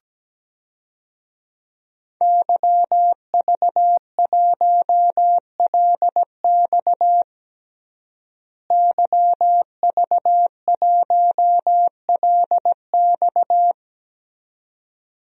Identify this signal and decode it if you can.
Morse code, heard as a keyed tone, about 17 wpm, 700 Hz. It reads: YV1LX YV1LX
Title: DXCC-100 Call Signs - 5 Characters - Repeated 2x - 17wpm